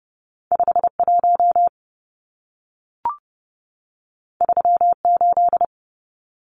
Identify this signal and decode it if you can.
Morse code, heard as a keyed tone, about 30 wpm, 700 Hz. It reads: 51 38
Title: Numbers - 2-Digit 30wpm